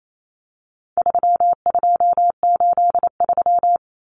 Morse code, heard as a keyed tone, about 28 wpm, 700 Hz. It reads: 3283
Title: Common Cut Numbers - 4-Digit 28wpm